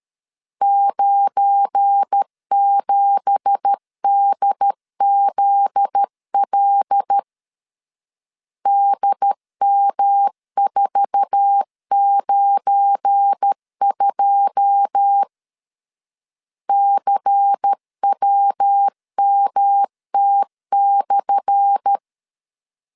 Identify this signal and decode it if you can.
Morse code, heard as a keyed tone, about 13 wpm, 800 Hz. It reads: 97DZL DM492 CWMT/
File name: LEZIONI / 10 SIGNS / 10A SLASH 12wpm 2314